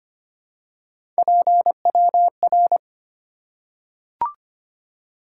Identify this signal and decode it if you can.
Morse code, heard as a keyed tone, about 25 wpm, 700 Hz. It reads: PWR
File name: Top 90 QSO Elements - 2x Character Spacing - 25wpm